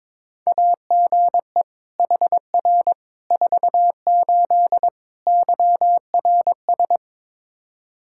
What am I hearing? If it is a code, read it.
Morse code, heard as a keyed tone, about 22 wpm, 700 Hz. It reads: AGE HR 48 YRS
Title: Practice QSO - Exchange 3 - More Info - 22wpm